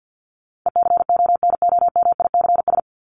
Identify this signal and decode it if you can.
Morse code, heard as a keyed tone, about 50 wpm, 700 Hz. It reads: ECONOMICS
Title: Top 501-750 Words 50wpm